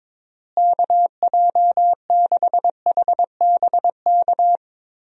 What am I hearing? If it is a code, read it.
Morse code, heard as a keyed tone, about 22 wpm, 700 Hz. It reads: KJ6HBK
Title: USA Call Signs - 6 Characters - Repeated 3x - 22wpm